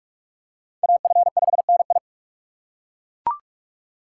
Morse code, heard as a keyed tone, about 40 wpm, 700 Hz. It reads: AF5NI